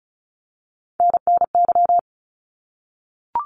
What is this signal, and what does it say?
Morse code, heard as a keyed tone, about 34 wpm, 700 Hz. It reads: NNY
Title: ARRL Sections 35wpm